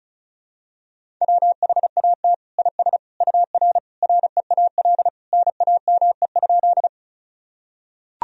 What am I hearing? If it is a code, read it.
Morse code, heard as a keyed tone, about 36 wpm, 700 Hz. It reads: WHAT IS UR REAL NAME?
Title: Sentences from Top 300 Words and Abbreviations 35wpm